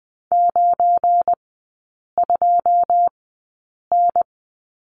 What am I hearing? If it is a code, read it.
Morse code, heard as a keyed tone, about 20 wpm, spaced as for 8 wpm, 690 Hz. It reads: 92N
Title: 191 Lesson 31e - Letter v - Three Characters 20wpm